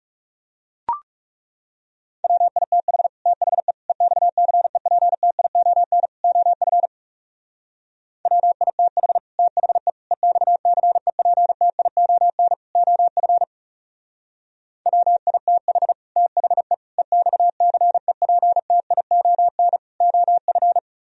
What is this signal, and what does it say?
Morse code, heard as a keyed tone, about 40 wpm, 700 Hz. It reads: WITH THE EXCEPTION OF WITH THE EXCEPTION OF WITH THE EXCEPTION OF